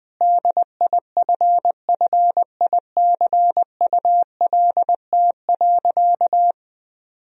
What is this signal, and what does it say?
Morse code, heard as a keyed tone, about 21 wpm, 700 Hz. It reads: DIFFICULT.
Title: Sentences from Top 1000 Words - Review Part 2 - 20wpm